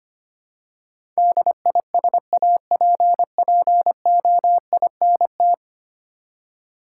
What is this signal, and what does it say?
Morse code, heard as a keyed tone, about 25 wpm, 690 Hz. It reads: DISAPPOINT
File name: 10 Letter Words - First 3 Letters Given - 25wpm